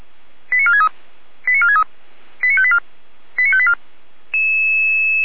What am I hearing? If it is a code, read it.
Five-tone selective call (ZVEI), heard as quick runs of five tones, tones of about 80 ms. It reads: 89463 89463 89584 89584
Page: /455/SDR/ZVEI/